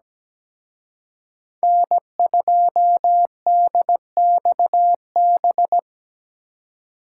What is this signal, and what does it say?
Morse code, heard as a keyed tone, about 20 wpm, 700 Hz. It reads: N2DXB
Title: Speed Racing - USA 1x3 Call Signs 17wpm